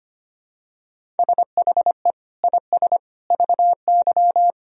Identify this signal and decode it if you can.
Morse code, heard as a keyed tone, about 25 wpm, 700 Hz. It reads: SHE IS VY
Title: Sentences from Top 400 Words and Abbreviations 25wpm